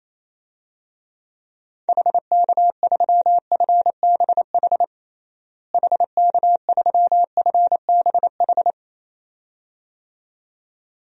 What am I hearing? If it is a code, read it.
Morse code, heard as a keyed tone, about 27 wpm, 700 Hz. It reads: HK3FBH HK3FBH
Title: DXCC-100 Call Signs - 6 Characters - Repeated 2x - 28wpm